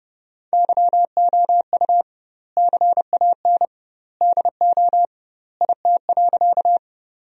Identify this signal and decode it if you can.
Morse code, heard as a keyed tone, about 31 wpm, 700 Hz. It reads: YOU CAN DO IT.
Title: 2x Word-Spacing - Sentences from Top 100 Words - 30wpm